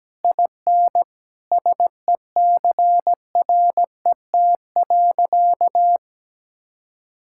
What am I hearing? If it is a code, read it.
Morse code, heard as a keyed tone, about 17 wpm, 700 Hz. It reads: IN SECRET.
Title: Sentences from Top 1600 Words 17wpm